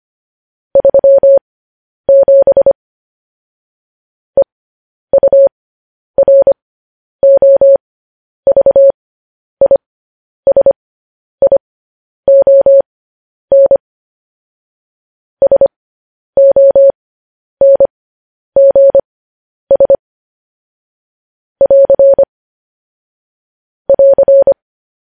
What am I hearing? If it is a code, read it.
Morse code, heard as a keyed tone, about 25 wpm, spaced as for 10 wpm, 550 Hz. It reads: 37 EUROVISION SONGS <AR> <AR>
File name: podcast / News headlines-1x-25x10